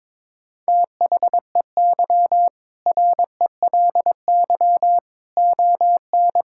Morse code, heard as a keyed tone, about 22 wpm, 700 Hz. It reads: THEY RELY ON
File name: practice / 22wpm / Sentences with 2-Word Phrases - 22wpm